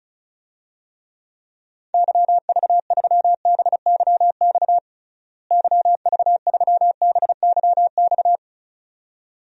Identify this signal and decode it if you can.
Morse code, heard as a keyed tone, about 36 wpm, 700 Hz. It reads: YV3BYX YV3BYX
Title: DXCC-100 Call Signs - 6 Characters - Repeated 2x - 35wpm